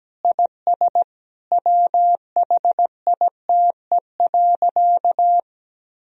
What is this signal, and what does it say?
Morse code, heard as a keyed tone, about 17 wpm, 700 Hz. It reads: IS WHITE.
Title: Sentences from Top 1000 Words - Review Part 1 - 17wpm